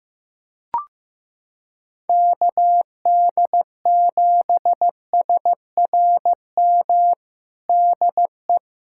Morse code, heard as a keyed tone, about 15 wpm, 700 Hz. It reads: KD7SRM DE
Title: USA Callsign DE Callsign 15wpm